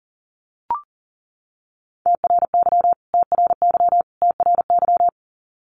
Morse code, heard as a keyed tone, about 40 wpm, 700 Hz. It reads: TRY TRY TRY
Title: Top 500 Words Repeated 40wpm